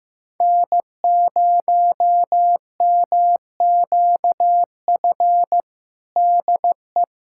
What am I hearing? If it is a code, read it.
Morse code, heard as a keyed tone, about 15 wpm, 700 Hz. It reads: N0MQF DE
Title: USA Callsign DE Callsign 15wpm